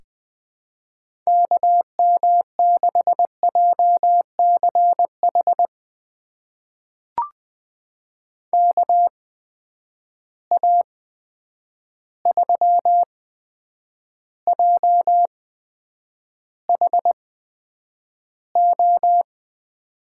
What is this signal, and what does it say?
Morse code, heard as a keyed tone, about 20 wpm, spaced as for 9 wpm, 700 Hz. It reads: KM6JCH KA3JHO
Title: USA Call Signs - 6 Characters - 8x Character Spacing - 20wpm